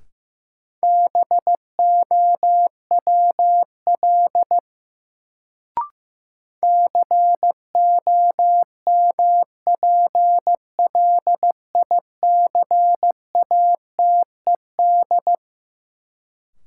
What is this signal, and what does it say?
Morse code, heard as a keyed tone, about 15 wpm, 700 Hz. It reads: BOWL COMPLICATED
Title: Sentences from Top 1200 Words - New Word Drill Alt - 15wpm